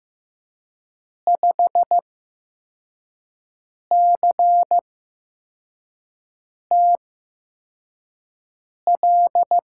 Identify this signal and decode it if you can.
Morse code, heard as a keyed tone, about 15 wpm, spaced as for 4 wpm, 700 Hz. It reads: 5CTL
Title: USA Call Signs - 6 Characters - 8x Character Spacing - 15wpm